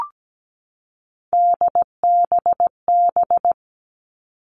Morse code, heard as a keyed tone, about 17 wpm, 700 Hz. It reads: DBB
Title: Three Challenging Letter-Number Combinations 17wpm